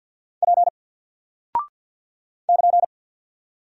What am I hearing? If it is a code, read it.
Morse code, heard as a keyed tone, about 50 wpm, 700 Hz. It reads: P <KN>
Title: Single Character (Letter:Number:Punctuation:Prosign) 50wpm